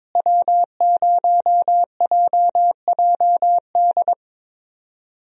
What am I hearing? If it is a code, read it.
Morse code, heard as a keyed tone, about 22 wpm, 700 Hz. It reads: W0JJD
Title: USA Call Signs - 5 Characters - Repeated 3x - 22wpm